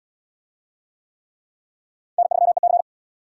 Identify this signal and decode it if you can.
Morse code, heard as a keyed tone, about 77 wpm, 700 Hz. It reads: N3EC